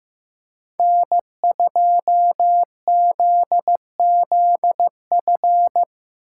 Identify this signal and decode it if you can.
Morse code, heard as a keyed tone, about 18 wpm, 700 Hz. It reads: N2ZZF
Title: Speed Racing - USA 1x3 Call Signs 15wpm